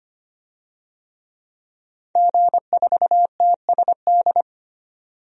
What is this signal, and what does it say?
Morse code, heard as a keyed tone, about 25 wpm, 690 Hz. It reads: G4TSD
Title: DXCC-100 Call Signs - 5 Characters - 25wpm